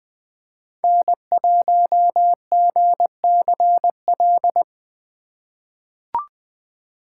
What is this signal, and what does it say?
Morse code, heard as a keyed tone, about 20 wpm, 700 Hz. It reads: N1GCL
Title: USA Call Signs - 5 Characters - 4x Character Spacing - 20wpm